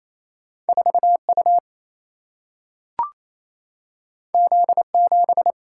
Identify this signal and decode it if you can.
Morse code, heard as a keyed tone, about 28 wpm, 700 Hz. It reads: 4 U Z 7
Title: Single-Letter Single-Number 28wpm